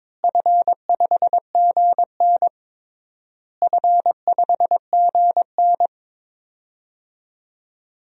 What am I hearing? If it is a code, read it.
Morse code, heard as a keyed tone, about 21 wpm, 700 Hz. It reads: F5GN F5GN
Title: DXCC-100 Call Signs - 4 Characters - Repeated 3x - 22wpm